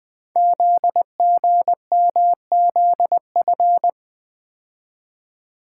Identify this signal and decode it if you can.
Morse code, heard as a keyed tone, about 20 wpm, 700 Hz. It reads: ZGMZF
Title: Code Groups - 5 Letters - 20wpm